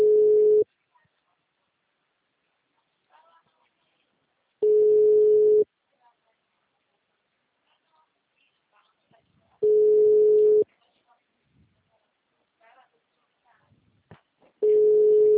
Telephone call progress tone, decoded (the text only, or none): ringback tone